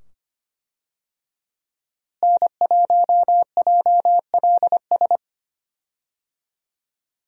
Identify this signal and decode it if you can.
Morse code, heard as a keyed tone, about 25 wpm, 700 Hz. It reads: N1JLS
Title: USA Call Signs - 5 Characters - Repeated 2x - 25wpm